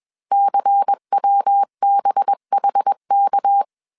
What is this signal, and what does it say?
Morse code, heard as a keyed tone, about 21 wpm, 800 Hz. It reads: CWBHK